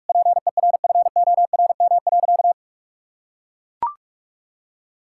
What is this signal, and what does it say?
Morse code, heard as a keyed tone, about 41 wpm, 700 Hz. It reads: PERFORM.
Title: Sentences from Top 1000 Words - Review Part 5 - 45wpm